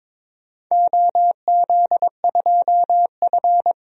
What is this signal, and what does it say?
Morse code, heard as a keyed tone, about 22 wpm, 700 Hz. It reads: OZ2F
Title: DXCC-100 Call Signs - 4 Characters - Repeated 3x - 22wpm